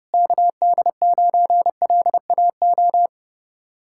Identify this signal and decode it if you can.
Morse code, heard as a keyed tone, about 30 wpm, 700 Hz. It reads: KD9LAO